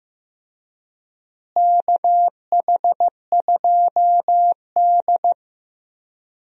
Morse code, heard as a keyed tone, about 15 wpm, 700 Hz. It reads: KH2D
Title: USA Call Signs - 4 Characters - Repeated 2x - 15wpm